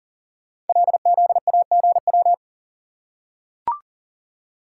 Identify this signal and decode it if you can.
Morse code, heard as a keyed tone, about 42 wpm, 700 Hz. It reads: LZAGW
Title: Code Groups - 5 Letter:Numbers - 40wpm